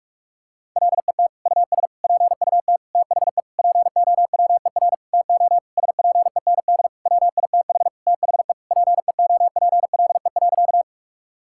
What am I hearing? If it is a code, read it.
Morse code, heard as a keyed tone, about 42 wpm, 700 Hz. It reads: LET US PUT THE POWER TO SPEND WITH THE PEOPLE.
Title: Sentences from Top 1000 Words - Review Part 5 - 45wpm